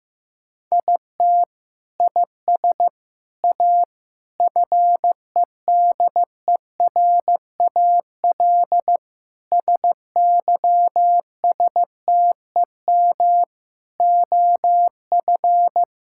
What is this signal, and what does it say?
Morse code, heard as a keyed tone, about 15 wpm, 700 Hz. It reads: IT IS A FEDERAL SYSTEM OF
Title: Sentences from Top 1000 Words - Review Part 5 - 15wpm